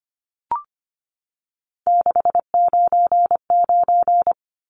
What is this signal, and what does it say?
Morse code, heard as a keyed tone, about 25 wpm, 700 Hz. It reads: 699
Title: Numbers - 3-Digit 25wpm